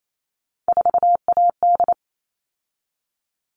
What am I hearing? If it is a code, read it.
Morse code, heard as a keyed tone, about 28 wpm, 700 Hz. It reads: 4AD
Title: All Cut Numbers - 3-Digit 28wpm